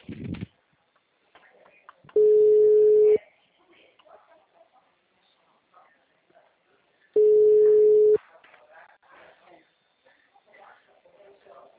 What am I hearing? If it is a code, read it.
Ringback tone (Europe, 425 Hz), with speech.